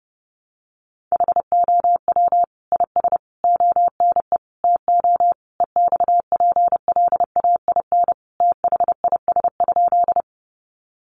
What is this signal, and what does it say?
Morse code, heard as a keyed tone, about 30 wpm, 700 Hz. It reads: HOW IS ONE TO EXPLAIN THIS?
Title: Sentences from Top 1100 Words 30wpm